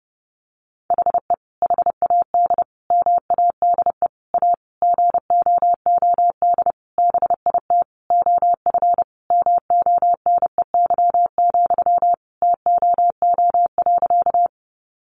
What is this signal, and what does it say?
Morse code, heard as a keyed tone, about 30 wpm, 700 Hz. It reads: HE HAD MADE A GOOD BIT OF MONEY, TOO.
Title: Sentences from Top 900 Words 30wpm